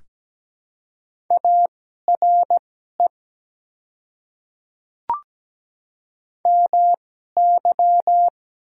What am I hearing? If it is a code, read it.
Morse code, heard as a keyed tone, about 17 wpm, spaced as for 12 wpm, 700 Hz. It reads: ARE MY
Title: 2x Character Spacing - Top 100 Words - 17wpm